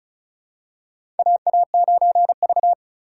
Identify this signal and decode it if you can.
Morse code, heard as a keyed tone, about 36 wpm, 690 Hz. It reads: AA9V
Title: USA Call Signs - 4 Characters - 2x Character Spacing - 35wpm